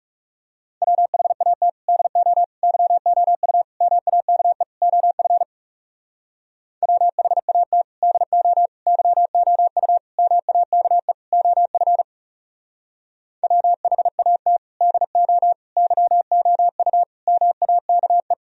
Morse code, heard as a keyed tone, about 38 wpm, 700 Hz. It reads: WHAT DO YOU MAKE OF WHAT DO YOU MAKE OF WHAT DO YOU MAKE